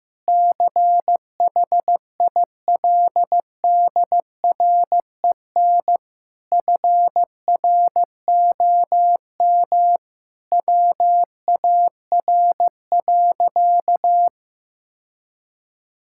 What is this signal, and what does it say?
Morse code, heard as a keyed tone, about 15 wpm, 700 Hz. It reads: CHILDREN FROM WAR.
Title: Sentences from Top 1000 Words - Review Part 6 - 15wpm